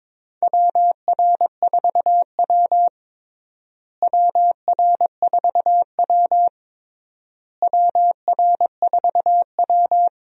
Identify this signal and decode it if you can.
Morse code, heard as a keyed tone, about 22 wpm, 700 Hz. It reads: WR4W WR4W WR4W
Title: USA Call Signs - 4 Characters - Repeated 3x - 22wpm